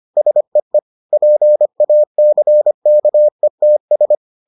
Morse code, heard as a keyed tone, about 25 wpm, 600 Hz. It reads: SEE PACKETS